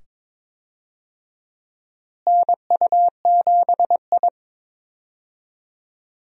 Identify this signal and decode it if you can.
Morse code, heard as a keyed tone, about 22 wpm, 700 Hz. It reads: NU7I
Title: USA Call Signs - 4 Characters - Repeated 2x - 22wpm